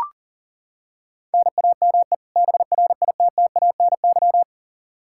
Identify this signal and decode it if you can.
Morse code, heard as a keyed tone, about 40 wpm, 700 Hz. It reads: NAME BRITTANY
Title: K1USN SST QSO Practice 40wpm